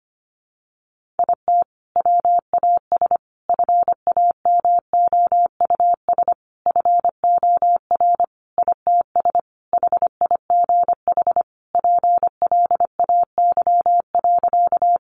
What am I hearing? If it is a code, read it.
Morse code, heard as a keyed tone, about 25 wpm, 700 Hz. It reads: IT WAS FAMOUS FOR ITS HIGH PLAY.
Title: Sentences from Top 1500 Words 25wpm